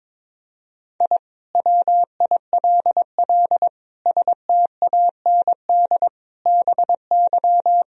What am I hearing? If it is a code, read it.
Morse code, heard as a keyed tone, about 22 wpm, 700 Hz. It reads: I WILL STAND BY
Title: Sentences from Top 1000 Words - Review Part 1 - 22wpm